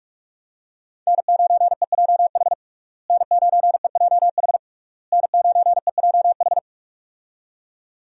Morse code, heard as a keyed tone, about 42 wpm, 700 Hz. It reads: N9EJH N9EJH N9EJH
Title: USA Call Signs - 5 Characters - Repeated 3x - 45wpm